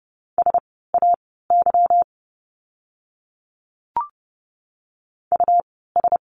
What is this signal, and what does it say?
Morse code, heard as a keyed tone, about 30 wpm, spaced as for 17 wpm, 700 Hz. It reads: SAY US
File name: course / 30:10wpm / 237 Lesson 38f - Letter q - Words 30wpm